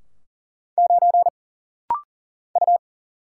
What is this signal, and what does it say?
Morse code, heard as a keyed tone, about 40 wpm, 700 Hz. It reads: 9 U
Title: Single Letter-Number Alt - Rapid-Fire 40wpm